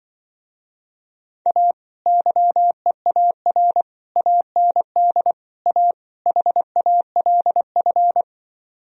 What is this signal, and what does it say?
Morse code, heard as a keyed tone, about 25 wpm, 700 Hz. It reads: A YEAR AND A HALF
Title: Speed-Racing - Sets of 5 Words 20wpm